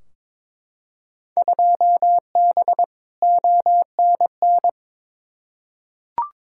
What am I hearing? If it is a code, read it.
Morse code, heard as a keyed tone, about 22 wpm, 700 Hz. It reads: 2B ONN
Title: Field Day Exchanges - 22wpm